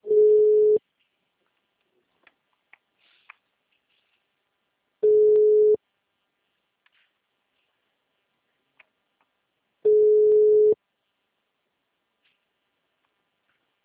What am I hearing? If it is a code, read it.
Ringback tone (Europe, 425 Hz).
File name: out- -2010- - - .87314